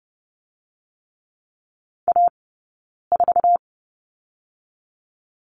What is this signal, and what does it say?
Morse code, heard as a keyed tone, about 30 wpm, 700 Hz. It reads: A 4
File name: Lesson 19d - Letter w - Two Characters 30wpm